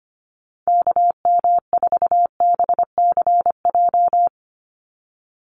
Morse code, heard as a keyed tone, about 25 wpm, 700 Hz. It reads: KM4BCJ
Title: USA Call Signs 25wpm